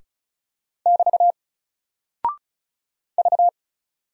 Morse code, heard as a keyed tone, about 36 wpm, 700 Hz. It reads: <BT> V
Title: Single Character (Letter:Number:Punctuation:Prosign) 35wpm